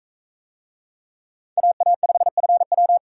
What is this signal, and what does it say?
Morse code, heard as a keyed tone, about 40 wpm, 700 Hz. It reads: AA5FW